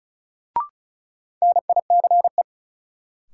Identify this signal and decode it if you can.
Morse code, heard as a keyed tone, about 34 wpm, 700 Hz. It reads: NICE
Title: Sentences from Top 1100 Words - New Word Drill Alt - 35wpm